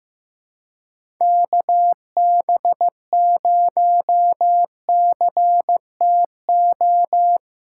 Morse code, heard as a keyed tone, about 16 wpm, 700 Hz. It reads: KB0CTO
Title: USA Call Signs - 6 Characters - 2x Character Spacing - 15wpm